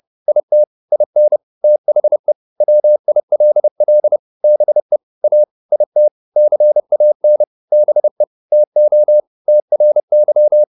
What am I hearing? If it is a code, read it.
Morse code, heard as a keyed tone, about 30 wpm, 600 Hz. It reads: IT IN THE WILL BE A IT CAN BE TO TRY